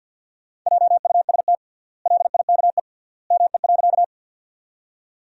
Morse code, heard as a keyed tone, about 47 wpm, 700 Hz. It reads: JUST LIKE ME.
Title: 3x Word-Spacing - Sentences from Top 200 Words - 50wpm